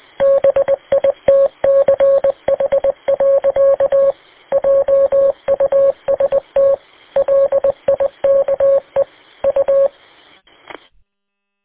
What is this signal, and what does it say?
Morse code, heard as a keyed tone, about 20 wpm, 550 Hz. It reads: BITCH. JUST LIKE U